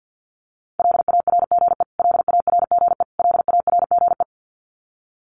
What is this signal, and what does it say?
Morse code, heard as a keyed tone, about 50 wpm, 700 Hz. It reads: LARGE LARGE LARGE
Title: Top 200 Words Repeated 50wpm